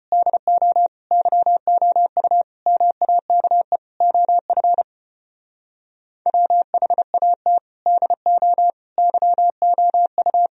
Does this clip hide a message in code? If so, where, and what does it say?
Yes: Morse code from 0.1 to 10.5 s, heard as a keyed tone, about 33 wpm, 700 Hz. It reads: DO YOU MAKE OF WHAT DO YOU